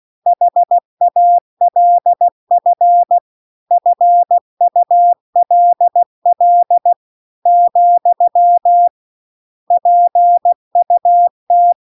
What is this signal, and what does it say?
Morse code, heard as a keyed tone, about 16 wpm, 700 Hz. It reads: HALF FULL , PUT